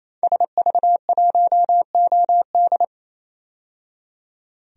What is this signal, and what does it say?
Morse code, heard as a keyed tone, about 28 wpm, 700 Hz. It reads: SV1OD
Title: DXCC-100 Call Signs - 5 Characters - 28wpm